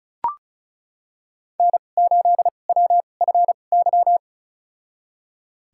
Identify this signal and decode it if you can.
Morse code, heard as a keyed tone, about 34 wpm, spaced as for 25 wpm, 700 Hz. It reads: N8WFY
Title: USA Call Signs - 5 Characters - 2x Character Spacing - 35wpm